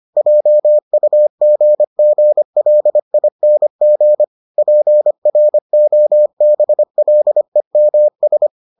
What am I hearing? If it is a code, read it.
Morse code, heard as a keyed tone, about 25 wpm, 600 Hz. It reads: JUGGLING PROBLEMS